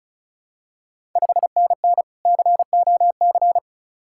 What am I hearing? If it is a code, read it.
Morse code, heard as a keyed tone, about 36 wpm, 690 Hz. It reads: 5NN COC
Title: Ontario QSO Party Exchange 35wpm